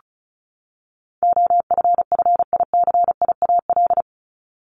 Morse code, heard as a keyed tone, about 34 wpm, 700 Hz. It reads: OFFICIAL
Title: Top 751-1000 Words 35wpm